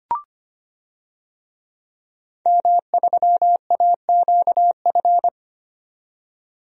Morse code, heard as a keyed tone, about 25 wpm, 700 Hz. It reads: M3AQF
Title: DXCC-100 Call Signs - 5 Characters - 25wpm